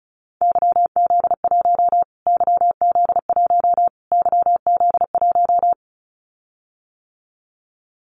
Morse code, heard as a keyed tone, about 34 wpm, 700 Hz. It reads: YZ1 YZ1 YZ1
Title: Callsign Prefixes for Top 25 DXCC - Repeated 3x - 35wpm